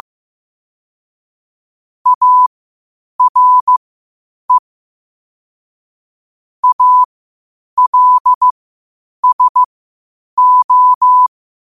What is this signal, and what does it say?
Morse code, heard as a keyed tone, about 15 wpm, spaced as for 8 wpm, 1000 Hz. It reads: ARE ALSO
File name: qotdfiles / 8wpm